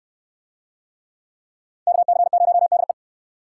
Morse code, heard as a keyed tone, about 68 wpm, 700 Hz. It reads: KC0DE